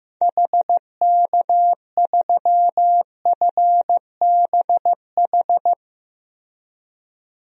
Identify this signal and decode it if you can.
Morse code, heard as a keyed tone, about 15 wpm, 700 Hz. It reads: HK3FBH
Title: DXCC-100 Call Signs - 6 Characters - Repeated 2x - 15wpm